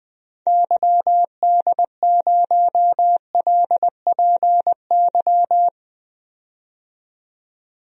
Morse code, heard as a keyed tone, about 20 wpm, 700 Hz. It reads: YD0LPY
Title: DXCC-100 Call Signs - 6 Characters - 20wpm